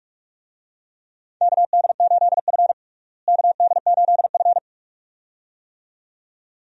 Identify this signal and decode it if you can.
Morse code, heard as a keyed tone, about 46 wpm, 700 Hz. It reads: KD8F KD8F
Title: USA Call Signs - 4 Characters - Repeated 2x - 45wpm